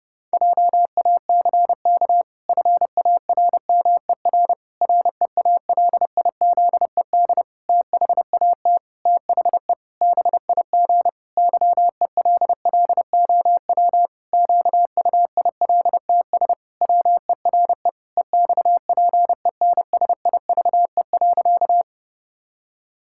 Morse code, heard as a keyed tone, about 30 wpm, 700 Hz. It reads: JACK FARMER REALIZED THAT THE BIG YELLOW QUILTS WERE EXPENSIVE.
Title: Pangram Challenge v2 - 30wpm